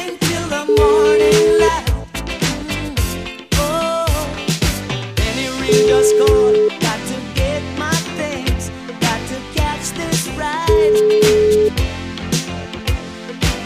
Ringback tone (Europe, 425 Hz), with noise.